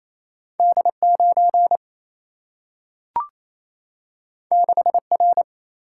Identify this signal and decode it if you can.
Morse code, heard as a keyed tone, about 28 wpm, 700 Hz. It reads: D 9 6 R
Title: Single-Letter Single-Number 28wpm